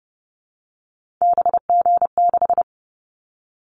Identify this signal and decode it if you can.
Morse code, heard as a keyed tone, about 30 wpm, 700 Hz. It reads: BG6
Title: Callsign Prefixes for Top 100 DXCC - 30wpm